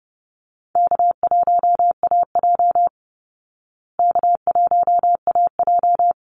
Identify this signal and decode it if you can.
Morse code, heard as a keyed tone, about 30 wpm, 700 Hz. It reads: K1AJ K1AJ
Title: Call Signs Repeated 3x - 30wpm